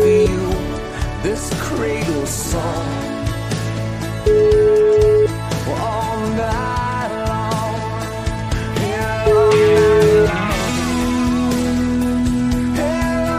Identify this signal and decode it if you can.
Ringback tone (Europe, 425 Hz), with noise.